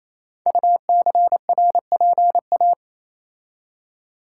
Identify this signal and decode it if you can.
Morse code, heard as a keyed tone, about 29 wpm, 700 Hz. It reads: UCRPA
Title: Code Groups - 5 Letter:Numbers - 28wpm